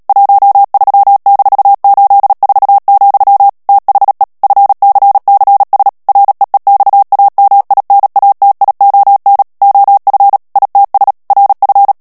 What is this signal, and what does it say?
Morse code, heard as a keyed tone, about 37 wpm, 800 Hz. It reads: 13-84, THE FCCS REEXAMINATION OF ITS RF
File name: morse mp3 / ANGLAIS texte clair 3 vitesse 35 mots minute